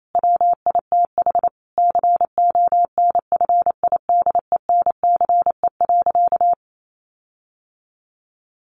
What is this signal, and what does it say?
Morse code, heard as a keyed tone, about 28 wpm, 700 Hz. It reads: WITH CONFIDENCE.
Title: Sentences from Top 1500 Words 28wpm